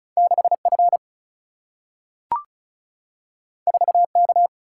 Morse code, heard as a keyed tone, about 36 wpm, 700 Hz. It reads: 6F 4K
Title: Single-Letter Single-Number 35wpm